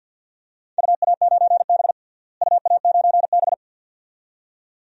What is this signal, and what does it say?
Morse code, heard as a keyed tone, about 52 wpm, 700 Hz. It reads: UA9B UA9B
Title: DXCC-100 Call Signs - 4 Characters - Repeated 2x - 50wpm